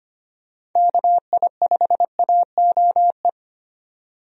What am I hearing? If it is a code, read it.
Morse code, heard as a keyed tone, about 25 wpm, 700 Hz. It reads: KI5AOE
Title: USA Call Signs - 6 Characters - 25wpm